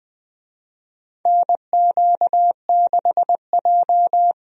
Morse code, heard as a keyed tone, about 20 wpm, 700 Hz. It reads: NQ6J
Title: USA Call Signs - 4 Characters - 20wpm